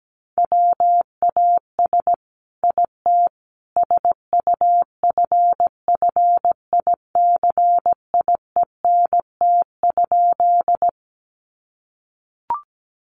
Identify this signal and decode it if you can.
Morse code, heard as a keyed tone, about 17 wpm, 700 Hz. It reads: WAS IT SUFFICIENT?
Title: Sentences from Top 1400 Words 17wpm